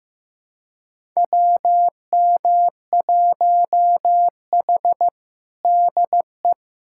Morse code, heard as a keyed tone, about 16 wpm, 700 Hz. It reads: WM1H DE
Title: USA Callsign DE Callsign - 4 Characters - 15wpm